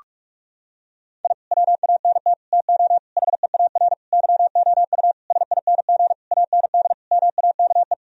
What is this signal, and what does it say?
Morse code, heard as a keyed tone, about 46 wpm, 700 Hz. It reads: I WANT TO HEAR YOU SING AND MAKE
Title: Sentences from Top 1000 Words - Review Part 4 - 45wpm